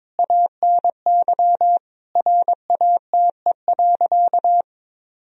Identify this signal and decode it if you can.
Morse code, heard as a keyed tone, about 22 wpm, 700 Hz. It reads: ANY RATE.
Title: Sentences from Top 1000 Words - Review Part 4 - 22wpm